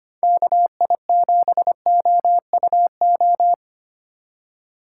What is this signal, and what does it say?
Morse code, heard as a keyed tone, about 25 wpm, 700 Hz. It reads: KI7OUO
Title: USA Call Signs - 6 Characters - 2x Character Spacing - 25wpm